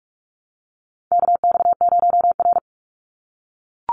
Morse code, heard as a keyed tone, about 44 wpm, 700 Hz. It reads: KX0R
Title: NA SOTA Activator Callsigns 45wpm